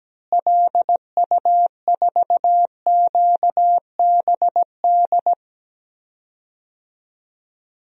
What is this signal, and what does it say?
Morse code, heard as a keyed tone, about 17 wpm, 700 Hz. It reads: LU4QBD
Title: DXCC-100 Call Signs - 6 Characters - Repeated 3x - 17wpm